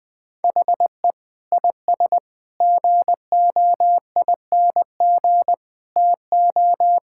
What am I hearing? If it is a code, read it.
Morse code, heard as a keyed tone, about 20 wpm, 700 Hz. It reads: HE IS GOING TO
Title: Sentences with 3-Word Phrases - 20wpm